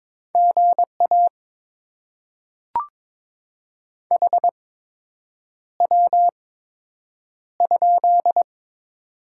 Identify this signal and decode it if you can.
Morse code, heard as a keyed tone, about 22 wpm, spaced as for 8 wpm, 700 Hz. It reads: GA HW?
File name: Top 20 QSO Elements - 8x Character Spacing - 22wpm